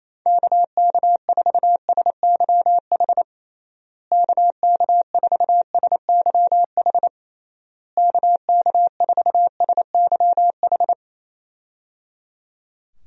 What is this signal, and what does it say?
Morse code, heard as a keyed tone, about 28 wpm, 700 Hz. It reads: KK4SYH KK4SYH KK4SYH
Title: USA Call Signs - 6 Characters - Repeated 3x - 28wpm